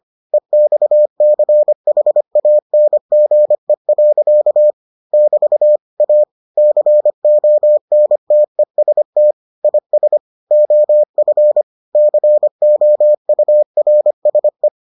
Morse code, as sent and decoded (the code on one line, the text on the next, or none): . -..- -.-. .... .- -. --. .-.-.- / -...- / .- / -.-. --- -. - . ... - / .. ... / --- ..-. / -.-. --- ..- .-. ... .
EXCHANGE. <BT> A CONTEST IS OF COURSE